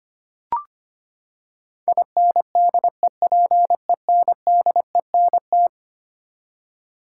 Morse code, heard as a keyed tone, about 25 wpm, 700 Hz. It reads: INDEPENDENT
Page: morse-code-ninja-practice-25wpm